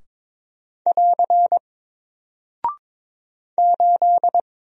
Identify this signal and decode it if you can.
Morse code, heard as a keyed tone, about 22 wpm, 700 Hz. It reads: <AR> 8
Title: Single Character (Letter:Number:Punctuation:Prosign) 22wpm